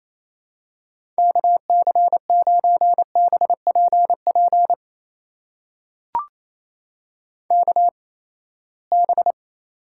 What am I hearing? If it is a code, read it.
Morse code, heard as a keyed tone, about 27 wpm, spaced as for 19 wpm, 700 Hz. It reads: KC9BPP KB
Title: USA Call Signs - 6 Characters - 8x Character Spacing - 28wpm